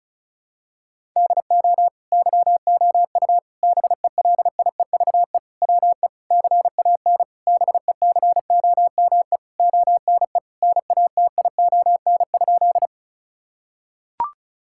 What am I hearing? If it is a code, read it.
Morse code, heard as a keyed tone, about 34 wpm, 700 Hz. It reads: DO YOU BELIEVE WE CAN BECOME ONE NATION?
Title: Sentences from Top 1000 Words - Review Part 6 - 35wpm